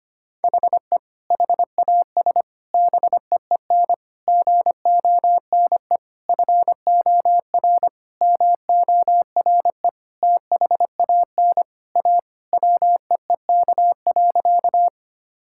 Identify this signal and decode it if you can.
Morse code, heard as a keyed tone, about 25 wpm, 700 Hz. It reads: HE HAS BEEN GONE FOR MORE THAN A WEEK.
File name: practice / 25wpm / Sentences with 4-Word Phrases - 25wpm